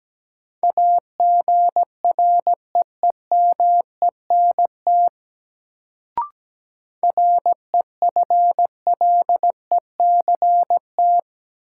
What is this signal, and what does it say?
Morse code, heard as a keyed tone, about 17 wpm, 700 Hz. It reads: AGREEMENT REFLECT
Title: Sentences from Top 1500 Words - New Word Drill Alt - 17wpm